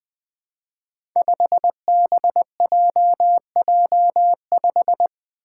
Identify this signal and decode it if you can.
Morse code, heard as a keyed tone, about 21 wpm, 700 Hz. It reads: 5BJJ5
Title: Five Challenging Letter-Number Combinations 20wpm